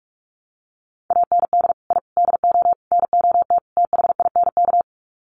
Morse code, heard as a keyed tone, about 44 wpm, 700 Hz. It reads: AND I DO NOT THINK
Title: Sets of 5 Words 45wpm